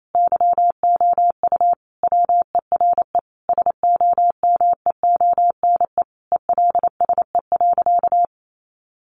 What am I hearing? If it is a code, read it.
Morse code, heard as a keyed tone, about 28 wpm, 700 Hz. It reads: YOU WERE SOMEONE ELSE.
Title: Sentences from Top 800 Words 28wpm